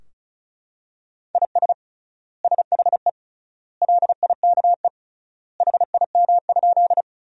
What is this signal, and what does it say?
Morse code, heard as a keyed tone, about 36 wpm, 700 Hz. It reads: IS SHE LIKE HIM?
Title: 3x Word-Spacing - Sentences from Top 100 Words - 35wpm